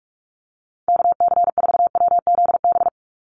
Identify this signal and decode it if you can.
Morse code, heard as a keyed tone, about 44 wpm, 700 Hz. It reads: KC4WZB
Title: NA SOTA Activator Callsigns 45wpm